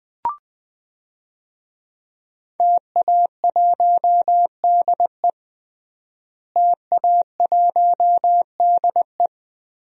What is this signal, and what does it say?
Morse code, heard as a keyed tone, about 20 wpm, 700 Hz. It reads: TA1DE TA1DE
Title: DXCC-100 Call Signs - 5 Characters - Repeated 2x - 20wpm